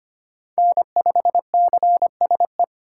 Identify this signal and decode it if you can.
Morse code, heard as a keyed tone, about 25 wpm, 700 Hz. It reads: N5CSE
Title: USA Call Signs - 5 Characters - 2x Character Spacing - 25wpm